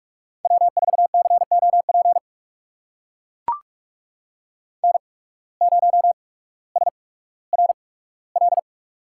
Morse code, heard as a keyed tone, about 44 wpm, spaced as for 19 wpm, 700 Hz. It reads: W4COP N0SRL
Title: USA Call Signs - 5 Characters - 8x Character Spacing - 45wpm